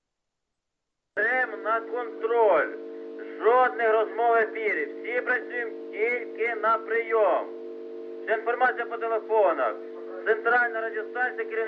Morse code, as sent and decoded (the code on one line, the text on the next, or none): none